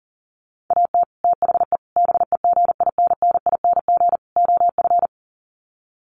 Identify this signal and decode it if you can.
Morse code, heard as a keyed tone, about 40 wpm, 700 Hz. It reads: AT THE BEGINNING OF
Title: Sets of 4 Words 40wpm